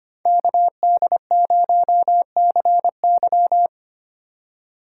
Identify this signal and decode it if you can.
Morse code, heard as a keyed tone, about 32 wpm, 700 Hz. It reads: KD0CY